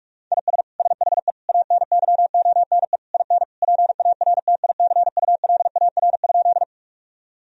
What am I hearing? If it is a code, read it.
Morse code, heard as a keyed tone, about 41 wpm, 700 Hz. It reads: IS SHE ANYONE IN PARTICULAR?
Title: Sentences from Top 1000 Words - Review Part 4 - 45wpm